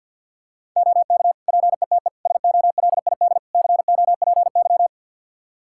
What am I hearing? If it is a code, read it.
Morse code, heard as a keyed tone, about 51 wpm, 700 Hz. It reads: OK PETE SOLID COPY
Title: Practice QSO - Exchange 3 - More Info - 50wpm